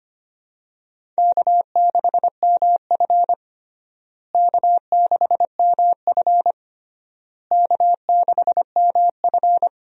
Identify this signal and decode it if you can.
Morse code, heard as a keyed tone, about 25 wpm, 700 Hz. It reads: K6MF K6MF K6MF
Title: USA Call Signs - 4 Characters - Repeated 3x - 25wpm